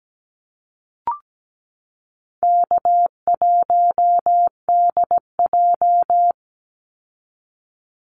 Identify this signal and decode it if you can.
Morse code, heard as a keyed tone, about 17 wpm, 700 Hz. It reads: K1DJ
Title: Easy Call Signs Repeated 2x - 17wpm